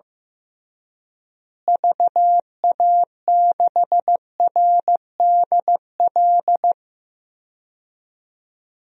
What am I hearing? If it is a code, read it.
Morse code, heard as a keyed tone, about 15 wpm, 700 Hz. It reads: VA6RDL
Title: DXCC-100 Call Signs - 6 Characters - Repeated 3x - 15wpm